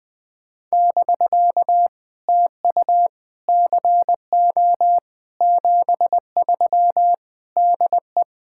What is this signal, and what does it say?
Morse code, heard as a keyed tone, about 20 wpm, 700 Hz. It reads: <BK> TU CO 73 DE
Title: POTA QSO Practice - 20wpm